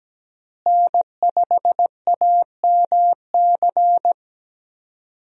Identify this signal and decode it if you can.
Morse code, heard as a keyed tone, about 17 wpm, 700 Hz. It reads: N5AMC
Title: USA Call Signs - 5 Characters - Repeated 2x - 17wpm